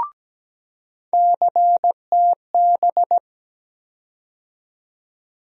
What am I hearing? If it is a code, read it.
Morse code, heard as a keyed tone, about 17 wpm, 700 Hz. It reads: CTB
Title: USA Call Sign Postfixes - 3 Letters 17wpm